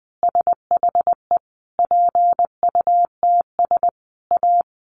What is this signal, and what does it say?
Morse code, heard as a keyed tone, about 20 wpm, 700 Hz. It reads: SHE PUTS A
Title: Easy Phrases 20wpm